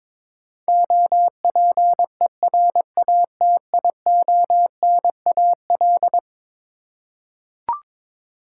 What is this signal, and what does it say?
Morse code, heard as a keyed tone, about 22 wpm, 700 Hz. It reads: OPERATIONAL
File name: practice / 22wpm / Eleven Letter Words - 22wpm